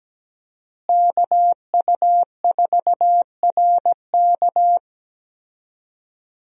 Morse code, heard as a keyed tone, about 17 wpm, 700 Hz. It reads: KU4RK
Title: USA Call Signs - 5 Characters - Repeated 2x - 17wpm